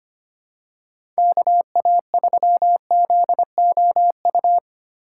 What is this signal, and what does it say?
Morse code, heard as a keyed tone, about 25 wpm, 700 Hz. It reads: KA3ZOU
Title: USA Call Signs - 8x Character Spacing - 25wpm